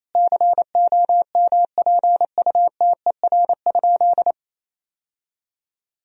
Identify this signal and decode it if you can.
Morse code, heard as a keyed tone, about 28 wpm, 700 Hz. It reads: COMPUTER?
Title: Sentences from Top 1000 Words - Review Part 3 - 28wpm